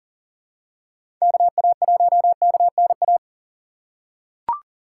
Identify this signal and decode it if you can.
Morse code, heard as a keyed tone, about 40 wpm, 700 Hz. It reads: KA1KNA
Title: USA Call Signs - 6 Characters - 3x Character Spacing - 40wpm